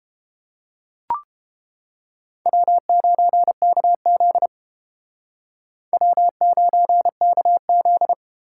Morse code, heard as a keyed tone, about 30 wpm, 700 Hz. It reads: W9KZ W9KZ